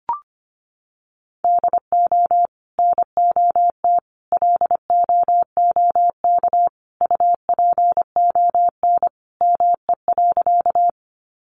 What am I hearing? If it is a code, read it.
Morse code, heard as a keyed tone, about 25 wpm, 700 Hz. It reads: DO NOT LOOK UPON ME.
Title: Sentences from Top 1100 Words 25wpm